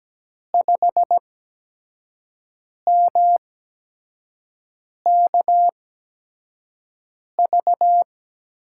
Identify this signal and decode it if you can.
Morse code, heard as a keyed tone, about 17 wpm, spaced as for 5 wpm, 700 Hz. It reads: 5MKV
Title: USA Call Signs - 8x Character Spacing - 17wpm